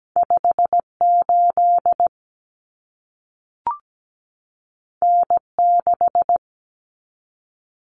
Morse code, heard as a keyed tone, about 17 wpm, 700 Hz. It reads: 58 N6
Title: Common Cut Numbers - 2-Digit 17wpm